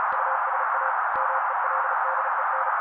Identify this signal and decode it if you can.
Morse code, heard as a keyed tone, about 37 wpm, 530 Hz. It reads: USA MELDEN
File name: daily / 35wpm noise